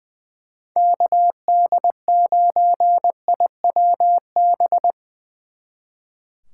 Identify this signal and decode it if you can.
Morse code, heard as a keyed tone, about 23 wpm, 700 Hz. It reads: KD9IWB
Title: Speed Racing - USA Call Signs 20wpm